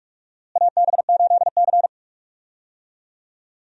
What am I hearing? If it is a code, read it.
Morse code, heard as a keyed tone, about 47 wpm, 700 Hz. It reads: AB8C
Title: USA Call Signs - 4 Characters - Repeated 2x - 45wpm